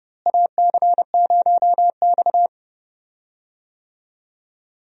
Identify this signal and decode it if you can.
Morse code, heard as a keyed tone, about 32 wpm, 700 Hz. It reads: AC0X
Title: USA Call Signs - 4 Characters - Repeated 3x - 30wpm